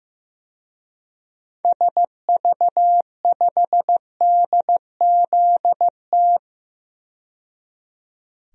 Morse code, heard as a keyed tone, about 16 wpm, 700 Hz. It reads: SV5DZT
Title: DXCC-100 Call Signs - 6 Characters - 15wpm